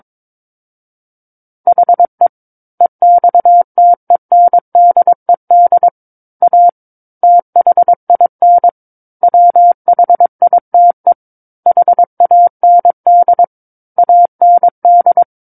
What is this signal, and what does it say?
Morse code, heard as a keyed tone, about 22 wpm, 700 Hz. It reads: HE EXTENDED A THIN WHITE HAND AND
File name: Book One / 13-Chapter 13 - 22 WPM